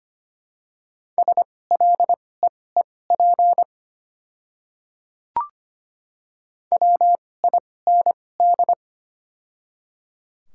Morse code, heard as a keyed tone, about 25 wpm, spaced as for 18 wpm, 690 Hz. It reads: SLEEP WIND
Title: 2x Character Spacing - Top 500 Words - 25wpm